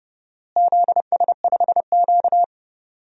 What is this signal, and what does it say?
Morse code, heard as a keyed tone, about 30 wpm, 700 Hz. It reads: ZS5Q
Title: DXCC-100 Call Signs - 4 Characters - Repeated 2x - 30wpm